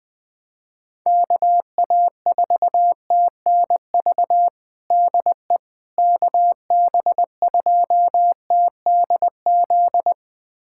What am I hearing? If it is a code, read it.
Morse code, heard as a keyed tone, about 20 wpm, 700 Hz. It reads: KA4TNV DE KB2TDZ
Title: USA Callsign DE Callsign - 6 Characters - 20wpm